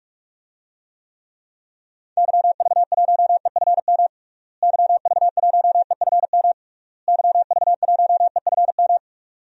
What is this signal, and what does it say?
Morse code, heard as a keyed tone, about 46 wpm, 700 Hz. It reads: YV1EFM YV1EFM YV1EFM
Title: DXCC-100 Call Signs - 6 Characters - Repeated 3x - 45wpm